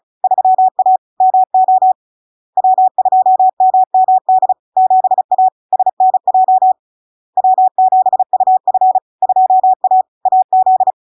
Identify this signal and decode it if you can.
Morse code, heard as a keyed tone, about 36 wpm, 750 Hz. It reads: 3A MO W2MMD 7A SNJ W7UF 2A AZ
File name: LARK-FD-2022-Log-in-CW-35wpm-750-Hz